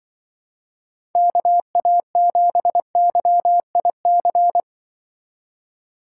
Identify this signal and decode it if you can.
Morse code, heard as a keyed tone, about 24 wpm, 700 Hz. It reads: KA7YIC
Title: Speed Racing - USA 2x3 Call Signs 20wpm